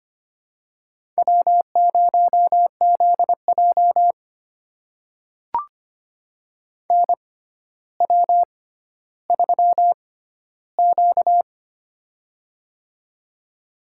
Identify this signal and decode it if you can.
Morse code, heard as a keyed tone, about 25 wpm, spaced as for 13 wpm, 700 Hz. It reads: W0ZJ NW3Q
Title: USA Call Signs - 4 Characters - 6x Character Spacing - 25wpm